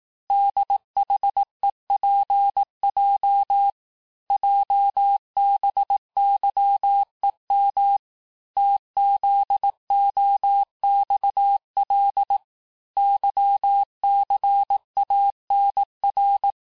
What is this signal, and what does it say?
Morse code, heard as a keyed tone, about 18 wpm, 800 Hz. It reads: DHEPJ JBYEM TZOXL YCANR
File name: cours-du-lundi files / cours-1594--teste-chiffre-vitesse-17mpm